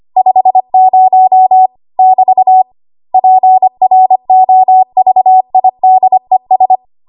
Morse code, heard as a keyed tone, about 25 wpm, 750 Hz. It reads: 50 <BT> PROVIDES
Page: W1AW Web Code Practice Files